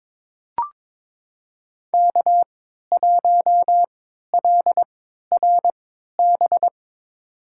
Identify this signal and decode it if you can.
Morse code, heard as a keyed tone, about 22 wpm, spaced as for 13 wpm, 700 Hz. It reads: K1LRB
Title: USA Call Signs - 5 Characters - 3x Character Spacing - 22wpm